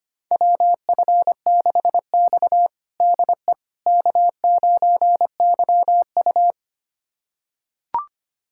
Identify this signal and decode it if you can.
Morse code, heard as a keyed tone, about 25 wpm, 700 Hz. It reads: WF6X DE K9YU
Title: USA Callsign DE Callsign - 4 Characters - 25wpm